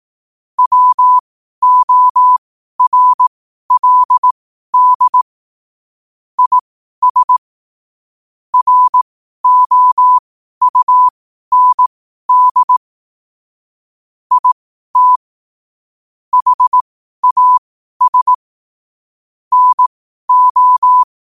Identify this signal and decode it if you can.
Morse code, heard as a keyed tone, about 18 wpm, spaced as for 12 wpm, 1000 Hz. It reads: WORLD IS ROUND IT HAS NO